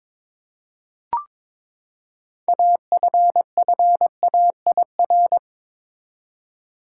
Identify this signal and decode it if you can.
Morse code, heard as a keyed tone, about 22 wpm, 700 Hz. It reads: AFFAIR
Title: Six Letter Words - 22wpm